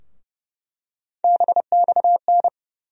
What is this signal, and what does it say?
Morse code, heard as a keyed tone, about 31 wpm, 700 Hz. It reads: BXN
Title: USA Call Sign Postfixes - 3 Letters 30wpm